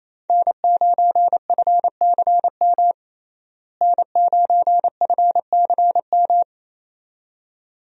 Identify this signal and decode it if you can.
Morse code, heard as a keyed tone, about 27 wpm, 700 Hz. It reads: N9FCM N9FCM
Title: USA Call Signs - 5 Characters - Repeated 2x - 28wpm